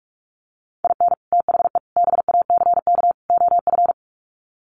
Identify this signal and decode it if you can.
Morse code, heard as a keyed tone, about 44 wpm, 700 Hz. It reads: IN THE BACK OF
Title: Sets of 4 Words 45wpm